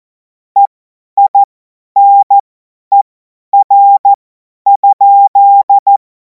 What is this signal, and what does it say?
Morse code, heard as a keyed tone, about 14 wpm, spaced as for 10 wpm, 800 Hz. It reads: EINER?
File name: Morsekurs / Witze / Witz 30. Juni 2026 50 F70